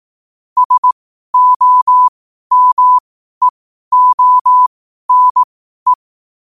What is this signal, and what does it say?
Morse code, heard as a keyed tone, about 18 wpm, spaced as for 12 wpm, 1000 Hz. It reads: SOMEONE
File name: qotdfiles / 12wpm